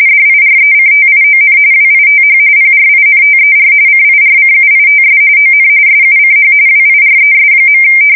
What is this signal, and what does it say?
RTTY, about 45.45 baud. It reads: DIRECTION WITH PRINTER CIRCUITS ON SUCH WIRES.
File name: fsk170 / printing-telegraph